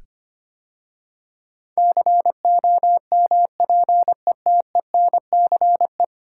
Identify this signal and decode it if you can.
Morse code, heard as a keyed tone, about 25 wpm, 700 Hz. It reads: COMPETENCE
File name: 10 Letter Words - First 3 Letters Given - 25wpm